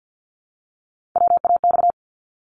Morse code, heard as a keyed tone, about 50 wpm, 700 Hz. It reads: WAX
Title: Multi-letter Phonemes Repeated 3x 50wpm